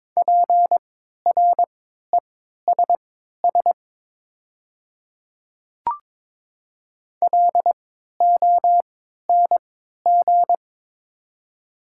Morse code, heard as a keyed tone, about 22 wpm, spaced as for 12 wpm, 700 Hz. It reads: PRESS LONG
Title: 3x Character Spacing - Top 300 Words - 22wpm